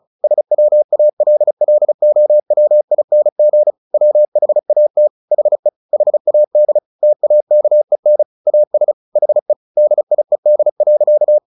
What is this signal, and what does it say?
Morse code, heard as a keyed tone, about 34 wpm, 600 Hz. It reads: SWALLOWING WHAT HE HAD TAKEN AS HE DIED.
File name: the old man and the sea 2 35wpm